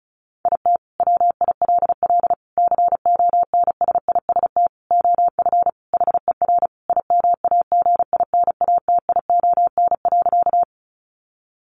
Morse code, heard as a keyed tone, about 34 wpm, 700 Hz. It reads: IT WILL CONSIST OF HER IMAGINATION.
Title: Sentences from Top 1400 Words 35wpm